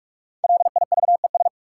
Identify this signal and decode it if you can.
Morse code, heard as a keyed tone, about 46 wpm, 700 Hz. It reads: LIVES
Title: Sets of 5 Words Encore - New Word Drill - 45wpm